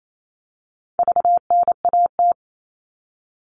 Morse code, heard as a keyed tone, about 28 wpm, 700 Hz. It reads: VNAT
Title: All Cut Numbers - 4-Digit 28wpm